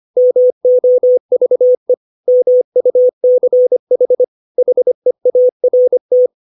Morse code, heard as a keyed tone, about 25 wpm, 500 Hz. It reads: MOVE MUCH HEART
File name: 632 words / 632 Words 25WPM